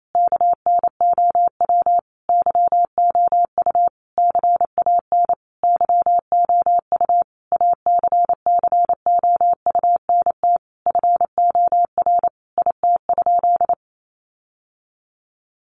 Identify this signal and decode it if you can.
Morse code, heard as a keyed tone, about 28 wpm, 700 Hz. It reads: KNOW YOU CAN YOU ACCOUNT FOR IT?
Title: Sentences from Top 900 Words 28wpm